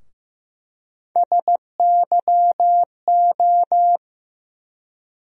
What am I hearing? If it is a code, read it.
Morse code, heard as a keyed tone, about 15 wpm, 700 Hz. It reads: SYO
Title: USA Call Sign Postfixes - 3 Letters 15wpm